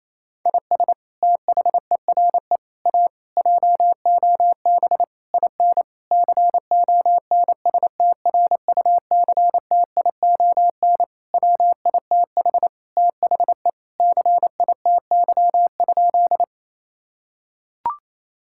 Morse code, heard as a keyed tone, about 28 wpm, 700 Hz. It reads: IS THERE A JOB IN CONSTRUCTION WITH THE CITY?
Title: Sentences from Top 1600 Words 28wpm